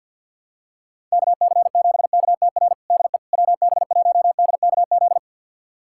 Morse code, heard as a keyed tone, about 49 wpm, 700 Hz. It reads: KC7KTR DE WB1DKZ
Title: USA Callsign DE Callsign 50wpm